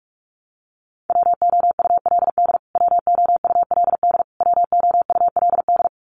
Morse code, heard as a keyed tone, about 44 wpm, 700 Hz. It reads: WOULD WOULD WOULD
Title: Top 500 Words Repeated 45wpm